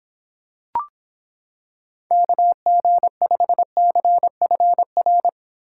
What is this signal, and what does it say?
Morse code, heard as a keyed tone, about 26 wpm, 700 Hz. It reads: KG5CFR